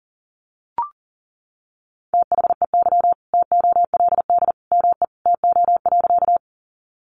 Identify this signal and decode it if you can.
Morse code, heard as a keyed tone, about 40 wpm, 700 Hz. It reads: THEY TOLD ME TO.
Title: Sentences from Top 400 Words 40wpm